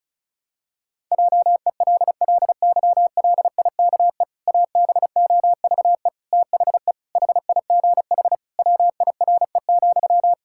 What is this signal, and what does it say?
Morse code, heard as a keyed tone, about 34 wpm, 700 Hz. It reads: JELLYLIKE ABOVE THE HIGH WIRE,
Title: Pangram Challenge v2 - 35wpm